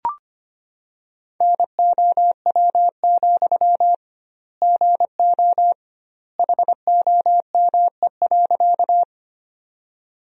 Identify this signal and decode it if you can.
Morse code, heard as a keyed tone, about 25 wpm, 700 Hz. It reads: NOW, GO HOME.
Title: 2x Word-Spacing - Sentences from Top 200 Words - 25wpm